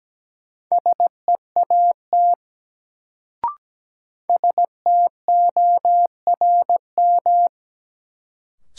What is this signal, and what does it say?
Morse code, heard as a keyed tone, about 17 wpm, 700 Hz. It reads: SEAT STORM
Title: Sentences from Top 1500 Words - New Word Drill - 17wpm